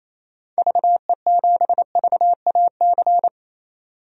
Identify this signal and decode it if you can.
Morse code, heard as a keyed tone, about 28 wpm, 700 Hz. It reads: VE7VAC
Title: DXCC-100 Call Signs - 6 Characters - 28wpm